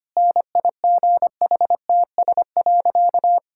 Morse code, heard as a keyed tone, about 25 wpm, 700 Hz. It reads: NIGHTS.
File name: practice / 25wpm / Difficult Sentences with Hints - 25wpm